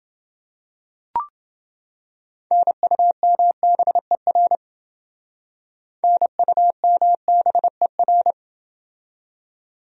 Morse code, heard as a keyed tone, about 29 wpm, 700 Hz. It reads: NUMBER NUMBER